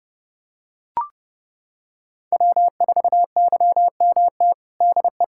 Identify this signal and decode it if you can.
Morse code, heard as a keyed tone, about 30 wpm, 700 Hz. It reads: W4YMT DE
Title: Practice QSO - Exchange 4 - Ending - 30wpm